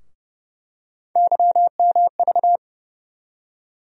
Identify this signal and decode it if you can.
Morse code, heard as a keyed tone, about 30 wpm, 700 Hz. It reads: YMV
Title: USA Call Sign Postfixes - 3 Letters 30wpm